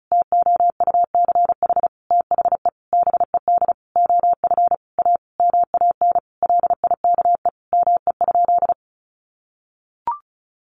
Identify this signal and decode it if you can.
Morse code, heard as a keyed tone, about 34 wpm, 700 Hz. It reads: TOUCH THE BED OF A MAN LIKE ME?
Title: Sentences from Top 1200 Words 35wpm